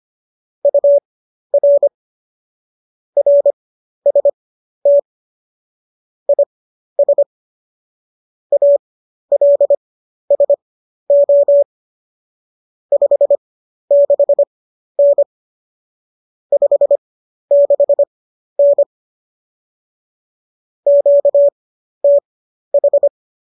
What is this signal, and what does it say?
Morse code, heard as a keyed tone, about 25 wpm, spaced as for 12 wpm, 570 Hz. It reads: UR RST IS ALSO 56N 56N QTH